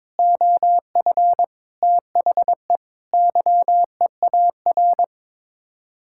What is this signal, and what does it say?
Morse code, heard as a keyed tone, about 22 wpm, 700 Hz. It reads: OF THE YEAR
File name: practice / 22wpm / Sets of 7 Words - 22wpm